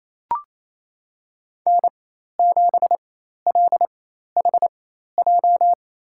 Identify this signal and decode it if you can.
Morse code, heard as a keyed tone, about 28 wpm, spaced as for 13 wpm, 700 Hz. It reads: N7LHJ
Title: USA Call Signs - 4x Character Spacing - 28wpm